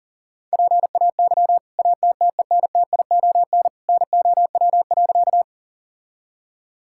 Morse code, heard as a keyed tone, about 40 wpm, 700 Hz. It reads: PAY ATTENTION NOW.
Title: Sentences from Top 1000 Words - Review Part 6 - 40wpm